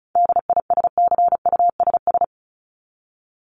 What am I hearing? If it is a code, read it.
Morse code, heard as a keyed tone, about 34 wpm, 700 Hz. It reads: DISCUSS
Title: ED Suffix - Top 100 Words with Base Words 35wpm